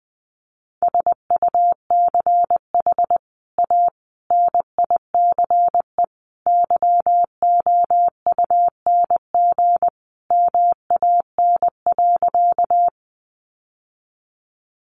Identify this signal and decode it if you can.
Morse code, heard as a keyed tone, about 20 wpm, 700 Hz. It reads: SUCH A NICE YOUNG MAN.
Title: Sentences from Top 1100 Words 20wpm